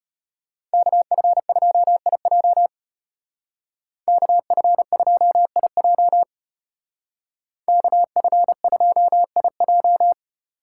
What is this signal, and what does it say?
Morse code, heard as a keyed tone, about 31 wpm, 700 Hz. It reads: KF2IJ KF2IJ KF2IJ